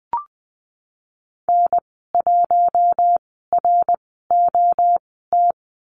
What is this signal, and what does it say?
Morse code, heard as a keyed tone, about 20 wpm, spaced as for 15 wpm, 700 Hz. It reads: N1ROT
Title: 055 Lesson 10g - Letter r - Call Signs 20wpm